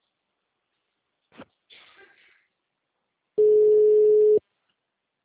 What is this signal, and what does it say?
Ringback tone (Europe, 425 Hz).
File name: out- -2036- - - .417